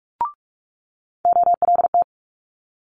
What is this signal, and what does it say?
Morse code, heard as a keyed tone, about 46 wpm, 700 Hz. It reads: OLT
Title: Multi-letter Phonemes Quick Fire 45wpm